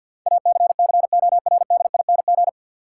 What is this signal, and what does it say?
Morse code, heard as a keyed tone, about 52 wpm, 700 Hz. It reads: ACCORDING
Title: Sentences from Top 700 Words - New Word Drill - 50wpm